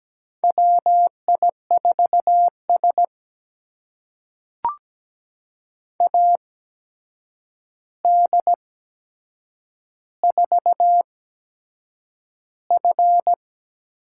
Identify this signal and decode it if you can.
Morse code, heard as a keyed tone, about 17 wpm, spaced as for 7 wpm, 700 Hz. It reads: WI4S AD4F
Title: USA Call Signs - 4 Characters - 8x Character Spacing - 17wpm